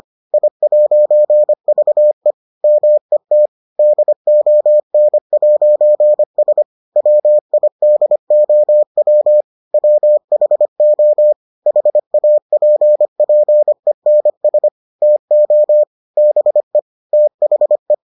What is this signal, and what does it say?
Morse code, heard as a keyed tone, about 25 wpm, 600 Hz. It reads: I'VE MET DON'S WIDOW WHO HAPPENS TO BE THE